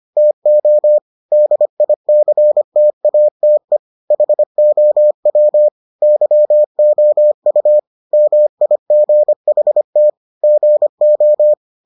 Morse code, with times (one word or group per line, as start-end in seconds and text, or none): 0.2-1.0 TO
1.3-3.8 DICTATE
4.1-5.7 HOW
6.0-7.8 YOU
8.1-10.1 MIGHT
10.4-11.5 GO